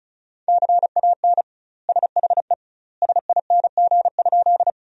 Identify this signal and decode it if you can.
Morse code, heard as a keyed tone, about 36 wpm, 700 Hz. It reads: CAN SHE SING?
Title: 2x Word-Spacing - Sentences from Top 400 Words - 35wpm